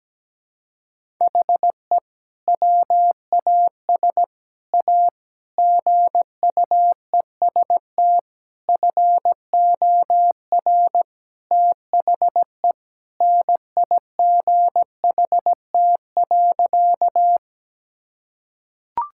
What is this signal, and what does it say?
Morse code, heard as a keyed tone, about 17 wpm, 700 Hz. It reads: HE WAS A GUEST FOR THE NIGHT.
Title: Sentences from Top 1600 Words 17wpm